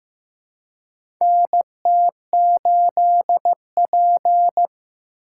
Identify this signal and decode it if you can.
Morse code, heard as a keyed tone, about 15 wpm, 700 Hz. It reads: NT8P
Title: USA Call Signs - 4 Characters - 2x Character Spacing - 15wpm